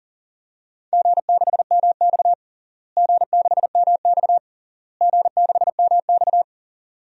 Morse code, heard as a keyed tone, about 40 wpm, 700 Hz. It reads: G6MX G6MX G6MX
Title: DXCC-100 Call Signs - 4 Characters - Repeated 3x - 40wpm